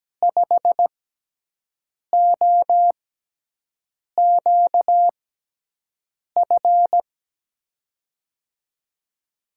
Morse code, heard as a keyed tone, about 17 wpm, spaced as for 6 wpm, 700 Hz. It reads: 5OQF
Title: USA Call Signs - 6 Characters - 6x Character Spacing - 17wpm